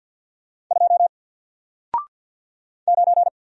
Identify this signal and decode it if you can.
Morse code, heard as a keyed tone, about 48 wpm, 700 Hz. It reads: 2 9
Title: Single Letter-Number Alt 50wpm